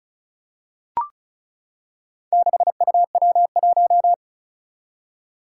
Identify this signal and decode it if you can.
Morse code, heard as a keyed tone, about 36 wpm, 700 Hz. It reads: 6UW1
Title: Four Challenging Letter-Number Combinations 35wpm